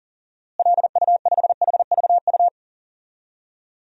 Morse code, heard as a keyed tone, about 40 wpm, 700 Hz. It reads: LU5HVU
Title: DXCC-100 Call Signs - 6 Characters - Repeated 2x - 40wpm